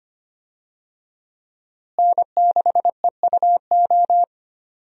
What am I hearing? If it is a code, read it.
Morse code, heard as a keyed tone, about 25 wpm, 700 Hz. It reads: N6EUO
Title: USA Call Signs - 5 Characters - Repeated 3x - 25wpm